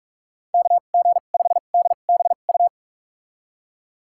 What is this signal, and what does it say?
Morse code, heard as a keyed tone, about 46 wpm, spaced as for 33 wpm, 700 Hz. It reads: KG5DBU
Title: USA Call Signs - 6 Characters - 2x Character Spacing - 45wpm